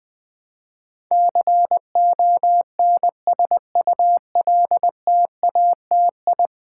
Morse code, heard as a keyed tone, about 20 wpm, 700 Hz. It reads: CONSULTATI
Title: Build A Word - 5 Letters At A Time - 12-Letter Words - 20wpm